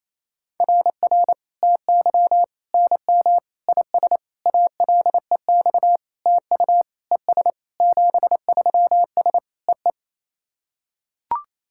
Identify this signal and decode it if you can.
Morse code, heard as a keyed tone, about 28 wpm, 690 Hz. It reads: RR TY NM IS ALEX TU ES 73S EE
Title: SOTA QSO Practice 28wpm